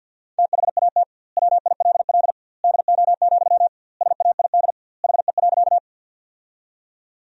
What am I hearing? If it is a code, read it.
Morse code, heard as a keyed tone, about 52 wpm, 700 Hz. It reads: THAT WILL DO, SAID HE.
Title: 2x Word-Spacing - Sentences from Top 100 Words - 50wpm